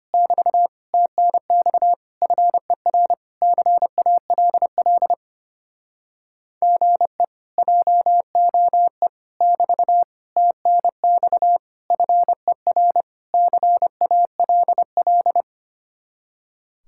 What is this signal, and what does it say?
Morse code, heard as a keyed tone, about 27 wpm, 700 Hz. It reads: <BT> TNX FER CALL GE JOE <BT> TNX FER CALL
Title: Speed-Racing Common QSO Phrases Encore - 25wpm